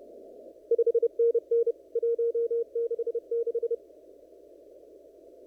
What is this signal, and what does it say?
Morse code, heard as a keyed tone, about 30 wpm, 470 Hz. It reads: 5NN 166
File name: CW / kt5j wpx cw-op-k5pi